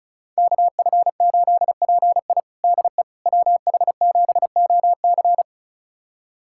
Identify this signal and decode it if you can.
Morse code, heard as a keyed tone, about 36 wpm, 700 Hz. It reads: KF8PI DE WH7OC
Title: USA Callsign DE Callsign - 5 Characters - 35wpm